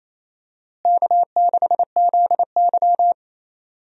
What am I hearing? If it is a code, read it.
Morse code, heard as a keyed tone, about 28 wpm, 700 Hz. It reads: K6ZY
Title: USA Call Signs - 4 Characters - Repeated 2x - 28wpm